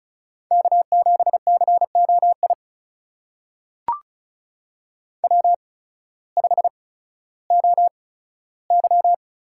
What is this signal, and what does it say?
Morse code, heard as a keyed tone, about 33 wpm, spaced as for 16 wpm, 700 Hz. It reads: K7COI W5OY
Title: USA Call Signs - 5 Characters - 8x Character Spacing - 35wpm